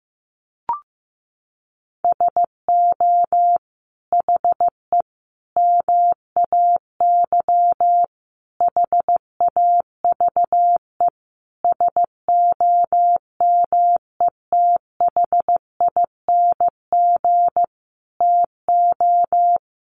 Morse code, heard as a keyed tone, about 15 wpm, 700 Hz. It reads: SO HE MAY HAVE SOMETHING TO
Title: Sentences from Top 1500 Words 15wpm